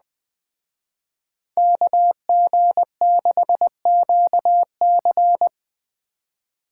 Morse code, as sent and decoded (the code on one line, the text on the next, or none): -.- --. -.... --.- -.-.
KG6QC